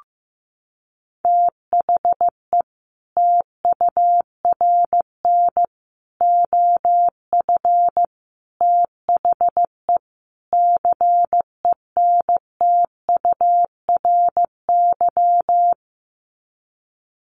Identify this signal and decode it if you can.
Morse code, heard as a keyed tone, about 15 wpm, 700 Hz. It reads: THE TURN OF THE CENTURY
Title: Sets of 5 Words 15wpm